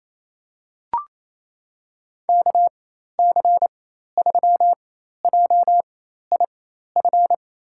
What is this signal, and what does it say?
Morse code, heard as a keyed tone, about 28 wpm, spaced as for 13 wpm, 700 Hz. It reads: KC3JIF
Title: USA Call Signs - 6 Characters - 4x Character Spacing - 28wpm